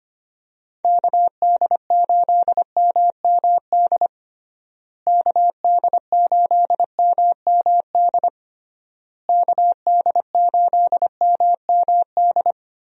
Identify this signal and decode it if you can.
Morse code, heard as a keyed tone, about 25 wpm, 700 Hz. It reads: KD8MMD KD8MMD KD8MMD
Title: USA Call Signs - 6 Characters - Repeated 3x - 25wpm